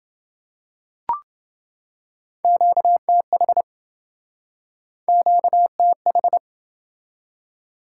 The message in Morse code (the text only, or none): QTH QTH